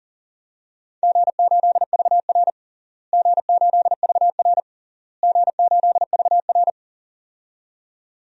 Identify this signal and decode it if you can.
Morse code, heard as a keyed tone, about 37 wpm, 700 Hz. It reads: G8VR G8VR G8VR
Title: DXCC-100 Call Signs - 4 Characters - Repeated 3x - 40wpm